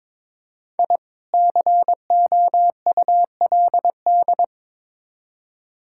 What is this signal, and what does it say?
Morse code, heard as a keyed tone, about 23 wpm, 700 Hz. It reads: I COULD
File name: Build A Phrase - 2-Word Phrases - 22wpm